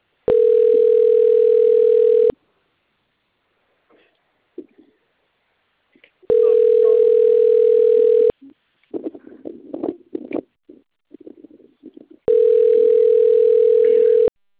Ringback tone (North America), 440 + 480 Hz.